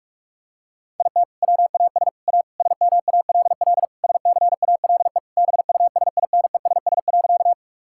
Morse code, heard as a keyed tone, about 46 wpm, 700 Hz. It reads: IT WAS A SMALL SCALE BUSINESS.
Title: Sentences from Top 1000 Words - Review Part 5 - 45wpm